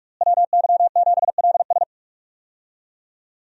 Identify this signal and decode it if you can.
Morse code, heard as a keyed tone, about 41 wpm, 700 Hz. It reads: WY7LS
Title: USA Call Signs - 5 Characters - Repeated 2x - 45wpm